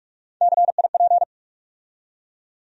Morse code, heard as a keyed tone, about 43 wpm, 700 Hz. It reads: CIP
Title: USA Call Sign Postfixes - 3 Letters 45wpm